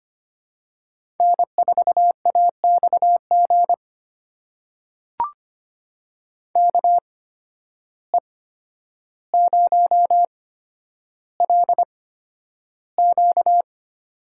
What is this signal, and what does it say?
Morse code, heard as a keyed tone, about 25 wpm, spaced as for 11 wpm, 700 Hz. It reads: N4AXG KE0LQ
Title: USA Call Signs - 5 Characters - 8x Character Spacing - 25wpm